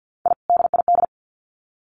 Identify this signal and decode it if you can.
Morse code, heard as a keyed tone, about 50 wpm, 700 Hz. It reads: I DID
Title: Sets of 2 Words 50wpm